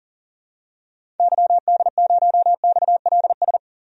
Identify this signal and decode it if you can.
Morse code, heard as a keyed tone, about 40 wpm, 700 Hz. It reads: YD0XLS
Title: DXCC-100 Call Signs - 6 Characters - 40wpm